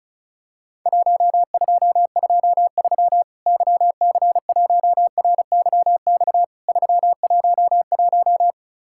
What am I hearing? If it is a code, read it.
Morse code, heard as a keyed tone, about 34 wpm, 700 Hz. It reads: 1223 YC1RYX 311
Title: Worked All Europe DX Contest - QTC Exchange - 35wpm